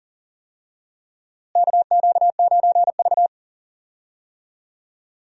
Morse code, heard as a keyed tone, about 40 wpm, 700 Hz. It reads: KQ9V
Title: USA Call Signs - 4 Characters - Repeated 3x - 40wpm